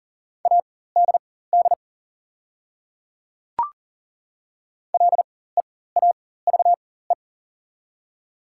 Morse code, heard as a keyed tone, about 40 wpm, spaced as for 19 wpm, 700 Hz. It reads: ADD LEAVE
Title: 4x Character Spacing - Top 500 Words - 40wpm